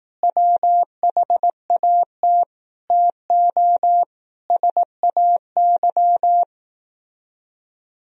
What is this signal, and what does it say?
Morse code, heard as a keyed tone, about 18 wpm, 700 Hz. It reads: WHAT TO SAY
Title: Speed-Racing - Sets of 5 Words 15wpm